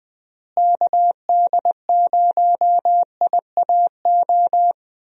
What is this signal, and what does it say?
Morse code, heard as a keyed tone, about 20 wpm, 700 Hz. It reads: KD0IAO
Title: USA Call Signs - 6 Characters - Repeated 3x - 20wpm